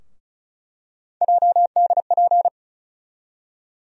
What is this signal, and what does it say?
Morse code, heard as a keyed tone, about 36 wpm, 700 Hz. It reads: JDP
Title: USA Call Sign Postfixes - 3 Letters 35wpm